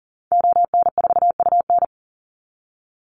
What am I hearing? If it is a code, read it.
Morse code, heard as a keyed tone, about 40 wpm, 700 Hz. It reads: ON4UN
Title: Call Signs Repeated 3x - 40wpm